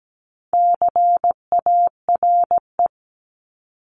Morse code, heard as a keyed tone, about 17 wpm, 700 Hz. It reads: CARE
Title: Top 500 Words 17wpm